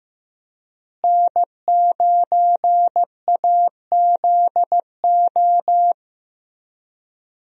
Morse code, heard as a keyed tone, about 15 wpm, 700 Hz. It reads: N9AZO
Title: USA Call Signs - 5 Characters - Repeated 2x - 15wpm